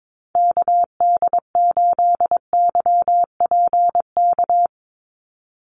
Morse code, heard as a keyed tone, about 22 wpm, 700 Hz. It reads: KD8YPK
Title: USA Call Signs 22wpm